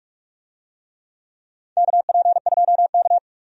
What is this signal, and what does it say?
Morse code, heard as a keyed tone, about 46 wpm, 700 Hz. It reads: KP2K
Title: DXCC-100 Call Signs - 4 Characters - 45wpm